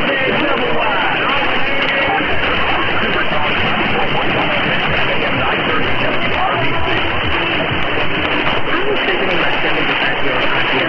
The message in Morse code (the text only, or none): none